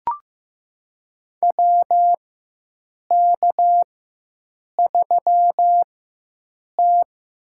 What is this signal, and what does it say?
Morse code, heard as a keyed tone, about 15 wpm, spaced as for 7 wpm, 700 Hz. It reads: WK3T
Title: USA Call Signs - 4 Characters - 4x Character Spacing - 15wpm